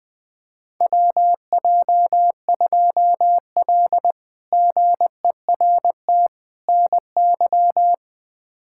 Morse code, heard as a keyed tone, about 20 wpm, 700 Hz. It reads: WJ2L GERT NY
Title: K1USN SST QSO Practice - 20wpm